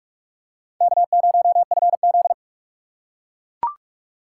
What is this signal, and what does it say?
Morse code, heard as a keyed tone, about 42 wpm, 690 Hz. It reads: K0FZ
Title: USA Call Signs - 4 Characters - 4x Character Spacing - 45wpm